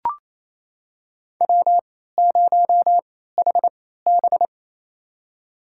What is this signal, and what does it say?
Morse code, heard as a keyed tone, about 27 wpm, spaced as for 16 wpm, 700 Hz. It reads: W0HB
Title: USA Call Signs - 3x Character Spacing - 28wpm